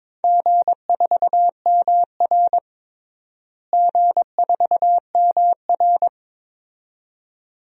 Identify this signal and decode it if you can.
Morse code, heard as a keyed tone, about 22 wpm, 700 Hz. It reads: G4MR G4MR
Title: DXCC-100 Call Signs - 4 Characters - Repeated 2x - 22wpm